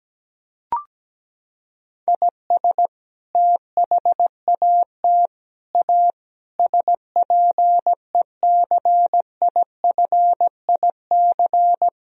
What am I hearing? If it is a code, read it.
Morse code, heard as a keyed tone, about 17 wpm, 700 Hz. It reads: IS THAT A SPECIFIC
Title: Sentences from Top 1000 Words - Review Part 3 - 17wpm